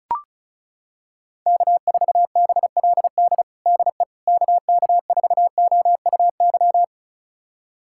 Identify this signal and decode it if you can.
Morse code, heard as a keyed tone, about 34 wpm, 700 Hz. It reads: K4BLD DE KK4OUY
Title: USA Callsign DE Callsign 35wpm